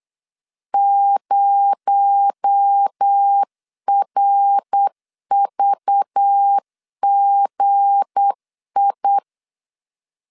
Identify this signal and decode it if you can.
Morse code, heard as a keyed tone, about 8 wpm, 800 Hz. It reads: 0RVGI
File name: LEZIONI / 7 GF9Z / 7 G 8wpm 3118